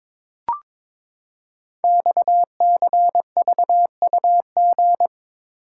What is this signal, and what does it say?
Morse code, heard as a keyed tone, about 22 wpm, 700 Hz. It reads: XCVUG
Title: Code Groups - 5 Letter:Numbers - 22wpm